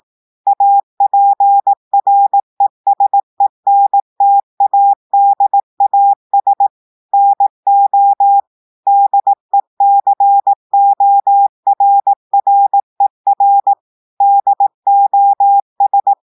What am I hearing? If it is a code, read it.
Morse code, heard as a keyed tone, about 18 wpm, 800 Hz. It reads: APRESENTADAS NO DECORRER DOS
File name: audios / portugais / portugais texte en clair 2 vitesse 18 mots minutes